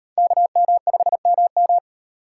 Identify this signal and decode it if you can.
Morse code, heard as a keyed tone, about 34 wpm, 700 Hz. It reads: KM5MM